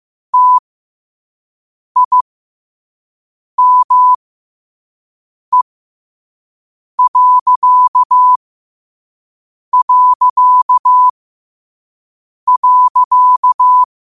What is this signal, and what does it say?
Morse code, heard as a keyed tone, about 15 wpm, spaced as for 5 wpm, 1000 Hz. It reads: TIME...
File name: qotdfiles / 5wpm